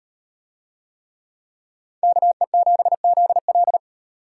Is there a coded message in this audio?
Yes: Morse code from 2.0 to 3.8 s, heard as a keyed tone, about 38 wpm, 700 Hz. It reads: KE7ZL